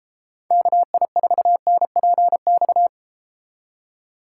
Morse code, heard as a keyed tone, about 33 wpm, 700 Hz. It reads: KI4NPX